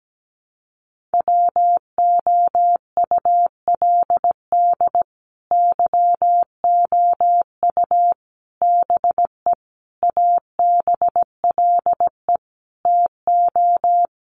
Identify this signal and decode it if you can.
Morse code, heard as a keyed tone, about 17 wpm, 700 Hz. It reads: WOULD YOU BE ABLE TO
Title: Sentences from Top 1400 Words 17wpm